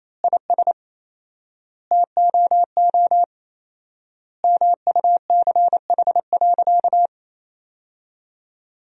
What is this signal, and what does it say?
Morse code, heard as a keyed tone, about 28 wpm, 700 Hz. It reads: IS TOO MUCH.
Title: 4x Word-Spacing - Sentences from Top 400 Words - 28wpm